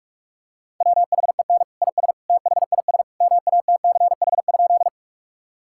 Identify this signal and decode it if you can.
Morse code, heard as a keyed tone, about 46 wpm, 700 Hz. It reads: WHEN IS THIS MATCH?
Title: Sentences from Top 1600 Words 45wpm